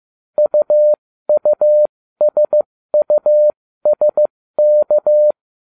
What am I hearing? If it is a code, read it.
Morse code, heard as a keyed tone, about 15 wpm, spaced as for 13 wpm, 600 Hz. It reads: UUSUSK